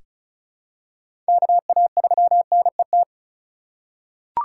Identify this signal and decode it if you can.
Morse code, heard as a keyed tone, about 36 wpm, 700 Hz. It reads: KA3NET
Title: USA Call Signs - 6 Characters - 3x Character Spacing - 35wpm